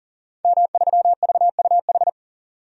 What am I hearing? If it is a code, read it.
Morse code, heard as a keyed tone, about 39 wpm, 700 Hz. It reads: M3VUH
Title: DXCC-100 Call Signs - 5 Characters - 40wpm